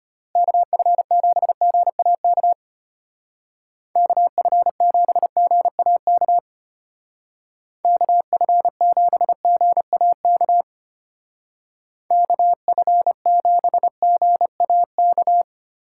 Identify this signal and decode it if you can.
Morse code, heard as a keyed tone, about 32 wpm, 700 Hz. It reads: KF7GAK KF7GAK KF7GAK KF7GAK